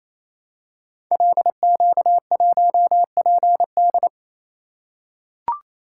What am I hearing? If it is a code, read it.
Morse code, heard as a keyed tone, about 27 wpm, 700 Hz. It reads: LQ1PD
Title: Code Groups - 5 Letter:Numbers - 28wpm